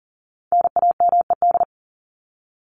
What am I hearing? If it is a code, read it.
Morse code, heard as a keyed tone, about 38 wpm, 700 Hz. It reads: NAMED
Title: ED Suffix - Top 100 Words with Base Words 40wpm